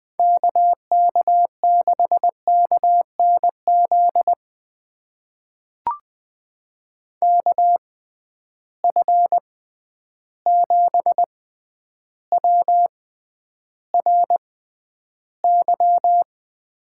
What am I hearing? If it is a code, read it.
Morse code, heard as a keyed tone, about 20 wpm, spaced as for 10 wpm, 700 Hz. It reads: KK6KNZ KF7WRY
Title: USA Call Signs - 6 Characters - 6x Character Spacing - 20wpm